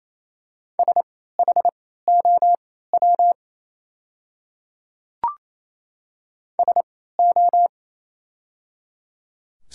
Morse code, heard as a keyed tone, about 28 wpm, spaced as for 16 wpm, 700 Hz. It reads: SHOW SO
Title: 3x Character Spacing - Top 200 Words - 28wpm